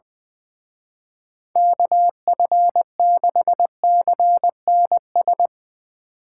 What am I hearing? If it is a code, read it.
Morse code, heard as a keyed tone, about 20 wpm, 700 Hz. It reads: KF6CNS
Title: Speed Racing - USA 2x3 Call Signs 20wpm